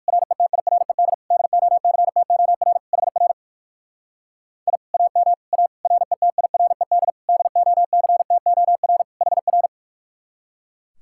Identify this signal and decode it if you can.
Morse code, heard as a keyed tone, about 59 wpm, 700 Hz. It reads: RETIRED DOCTOR HR I AM A RETIRED DOCTOR HR